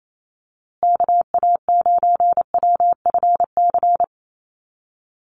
Morse code, heard as a keyed tone, about 28 wpm, 700 Hz. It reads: KA9WFC
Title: USA Call Signs Repeated 2x 28wpm